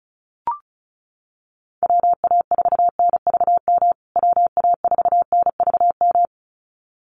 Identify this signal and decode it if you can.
Morse code, heard as a keyed tone, about 34 wpm, 700 Hz. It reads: WA4NVM WA4NVM
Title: USA Call Signs Repeated 2x 35wpm